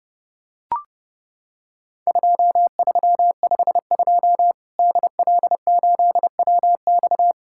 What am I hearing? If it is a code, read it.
Morse code, heard as a keyed tone, about 28 wpm, 700 Hz. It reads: 2352 DL8WX
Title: Worked All Europe DX Contest - QTC Exchange - 30wpm